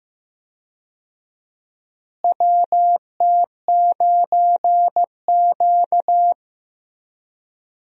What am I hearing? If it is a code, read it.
Morse code, heard as a keyed tone, about 15 wpm, 700 Hz. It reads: WT9Q
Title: USA Call Signs - 4 Characters - Repeated 3x - 15wpm